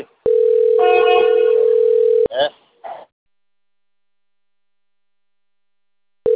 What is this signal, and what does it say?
Ringback tone (North America), 440 + 480 Hz.